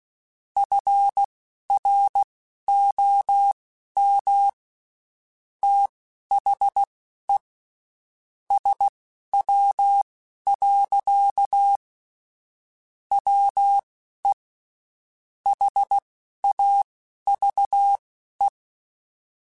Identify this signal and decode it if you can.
Morse code, heard as a keyed tone, about 16 wpm, spaced as for 11 wpm, 800 Hz. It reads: FROM THE SW. WE HAVE
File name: Spaced Characters / Spaced Characters-mp3s / Weather 5 5 sp @15